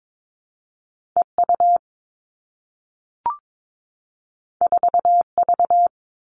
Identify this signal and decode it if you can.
Morse code, heard as a keyed tone, about 22 wpm, 700 Hz. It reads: E U 4 V
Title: Two Challenging Letter-Number Combinations 22wpm